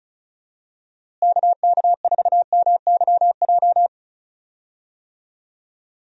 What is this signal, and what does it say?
Morse code, heard as a keyed tone, about 34 wpm, 700 Hz. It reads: KK4MYJ
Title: USA Call Signs - 6 Characters - Repeated 2x - 35wpm